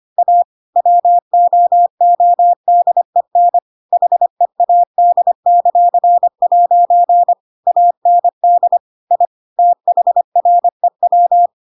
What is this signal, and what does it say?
Morse code, heard as a keyed tone, about 25 wpm, 700 Hz. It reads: A WOODEN HEAD;' AND I THREW
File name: Pinocchio / Pinocchio ch7 25 25 ik1hge